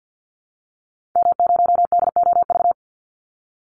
Morse code, heard as a keyed tone, about 48 wpm, 700 Hz. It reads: M0DOV
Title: Call Signs Repeated 2x - 50wpm